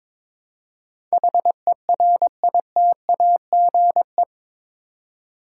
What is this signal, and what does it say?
Morse code, heard as a keyed tone, about 22 wpm, 700 Hz. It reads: HERITAGE
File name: practice / 22wpm / Eight Letter Words - 22wpm